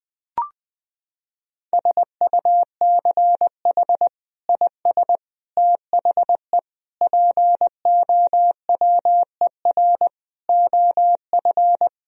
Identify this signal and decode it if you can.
Morse code, heard as a keyed tone, about 21 wpm, 700 Hz. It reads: SUCH IS THE POWER OF
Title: Sentences from Top 1000 Words - Review Part 2 - 20wpm